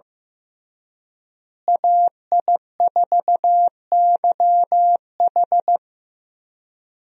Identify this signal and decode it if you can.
Morse code, heard as a keyed tone, about 15 wpm, 700 Hz. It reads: AI4YH
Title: USA Call Signs - 5 Characters - Repeated 2x - 15wpm